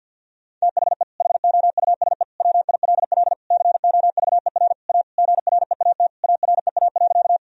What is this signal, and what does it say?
Morse code, heard as a keyed tone, about 52 wpm, 700 Hz. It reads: THE HOUSE WILL COVER A GREAT AREA.
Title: Sentences from Top 1000 Words - Review Part 2 - 50wpm